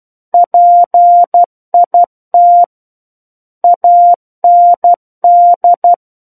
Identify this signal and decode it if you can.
Morse code, heard as a keyed tone, about 12 wpm, 700 Hz. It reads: PIT AND
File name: Book One / 03-Chapter 3 - 12 WPM